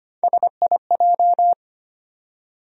Morse code, heard as a keyed tone, about 25 wpm, 700 Hz. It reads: SIJ
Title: USA Call Sign Postfixes - 3 Letters 25wpm